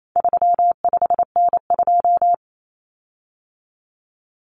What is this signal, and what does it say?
Morse code, heard as a keyed tone, about 28 wpm, 700 Hz. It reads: 35N2
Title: Common Cut Numbers - 4-Digit 28wpm